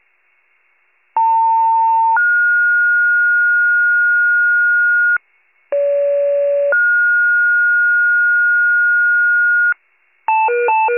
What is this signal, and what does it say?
Two-tone page, heard as a one-second tone, then a three-second tone, about 900 then 1430 Hz, about 570 then 1430 Hz. It is heard from 1.2 to 9.7 s.
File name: wcalert